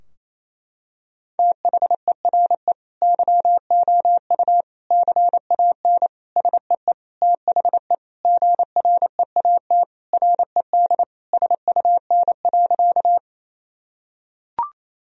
Morse code, heard as a keyed tone, about 28 wpm, 700 Hz. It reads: THERE YOU CAN SEE THE GREAT RED SUN.
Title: Sentences from Top 1000 Words - Review Part 2 - 28wpm